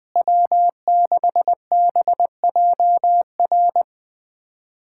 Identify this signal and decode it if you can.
Morse code, heard as a keyed tone, about 20 wpm, 700 Hz. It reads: W6BJR
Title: USA Callsign DE Callsign - 5 Characters - 20wpm